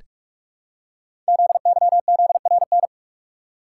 Morse code, heard as a keyed tone, about 44 wpm, 700 Hz. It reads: ZYZRN
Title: Code Groups - 5 Letters - 45wpm